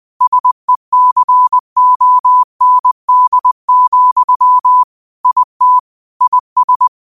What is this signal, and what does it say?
Morse code, heard as a keyed tone, about 20 wpm, 1000 Hz. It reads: SECOND, IT IS
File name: qotdfiles / 20wpm